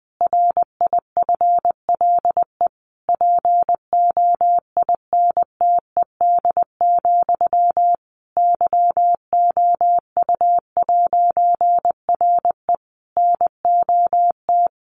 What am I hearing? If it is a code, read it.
Morse code, heard as a keyed tone, about 20 wpm, 700 Hz. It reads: RIFLE POINTED, YOU'RE NOT
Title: A Sound Of Thunder 20wpm